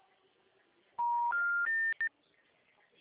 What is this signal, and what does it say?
Special information tone, with speech.
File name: out- -2017